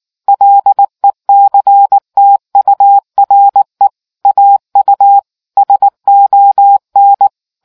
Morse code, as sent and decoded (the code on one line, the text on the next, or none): .-.. -.-. - ..- .-. / .- ..- / ... --- -.
LECTURE AU SON